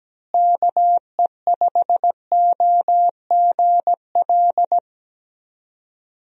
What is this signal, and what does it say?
Morse code, heard as a keyed tone, about 17 wpm, 700 Hz. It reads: KE5OGL
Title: USA Call Signs - 6 Characters - Repeated 2x - 17wpm